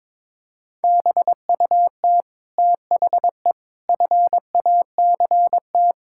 Morse code, heard as a keyed tone, about 22 wpm, 700 Hz. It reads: BUT THE FACT
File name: Sets of 9 Words - 22wpm